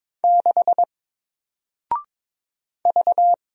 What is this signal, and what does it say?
Morse code, heard as a keyed tone, about 22 wpm, 700 Hz. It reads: 6 V
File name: Single-Letter-Number-22wpm